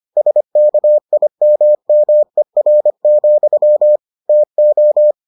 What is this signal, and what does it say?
Morse code, heard as a keyed tone, about 25 wpm, 600 Hz. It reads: SKIMMER, TO